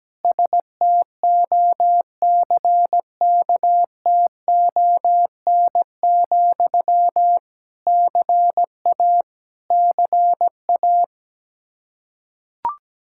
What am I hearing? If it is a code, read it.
Morse code, heard as a keyed tone, about 17 wpm, 700 Hz. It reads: STOCKTON, CA CA
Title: Practice QSO - Exchange 1 - Beginning - 17wpm